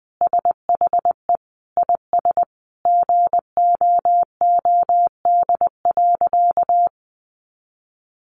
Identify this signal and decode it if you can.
Morse code, heard as a keyed tone, about 20 wpm, 700 Hz. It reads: SHE IS GOOD.
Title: Easy Phrases 20wpm